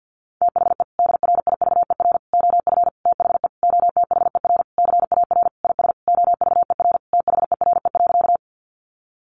Morse code, heard as a keyed tone, about 50 wpm, 700 Hz. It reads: THE DRIVER OF THE OTHER CAR IS OVER THERE.
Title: Sentences from Top 1000 Words 50wpm